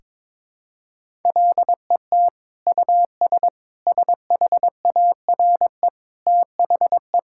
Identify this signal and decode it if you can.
Morse code, heard as a keyed tone, about 22 wpm, 700 Hz. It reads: LET US SHARE THE
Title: Sentences from Top 1000 Words - Review Part 6 - 22wpm